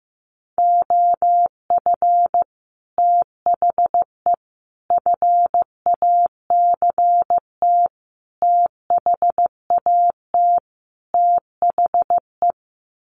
Morse code, heard as a keyed tone, about 15 wpm, 700 Hz. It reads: OF THE FACT THAT THE
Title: Sets of 5 Words Encore2 15wpm